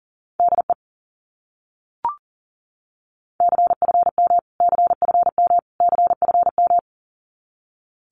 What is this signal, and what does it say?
Morse code, heard as a keyed tone, about 37 wpm, 700 Hz. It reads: DE CFM CFM CFM
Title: Abbreviations Repeated 3x - 40wpm